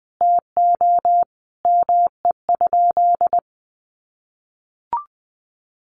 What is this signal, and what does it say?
Morse code, heard as a keyed tone, about 20 wpm, 700 Hz. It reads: TO ME?
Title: Sentences from Top 900 Words 20wpm